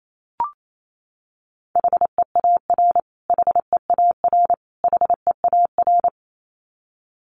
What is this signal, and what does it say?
Morse code, heard as a keyed tone, about 28 wpm, 700 Hz. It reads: HEAR HEAR HEAR
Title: Top 500 Words Repeated 28wpm